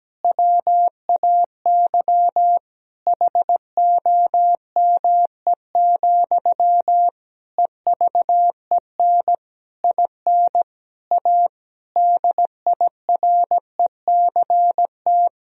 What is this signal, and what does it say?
Morse code, heard as a keyed tone, about 17 wpm, 700 Hz. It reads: WAY HOME, EVEN IN A DIRECT
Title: Sentences from Top 1000 Words - Review Part 2 - 17wpm